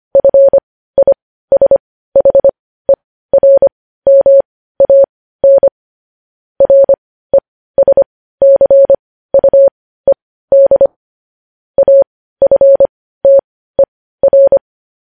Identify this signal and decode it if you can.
Morse code, heard as a keyed tone, about 25 wpm, spaced as for 15 wpm, 550 Hz. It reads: FISHERMAN RESCUED AFTER
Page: news-headlines-in-morse-code-at-15-wpm